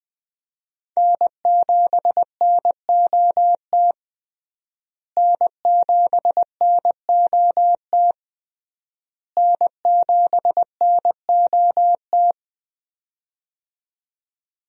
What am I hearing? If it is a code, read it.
Morse code, heard as a keyed tone, about 20 wpm, 700 Hz. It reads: N7NOT N7NOT N7NOT
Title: USA Call Signs - 5 Characters - Repeated 3x - 20wpm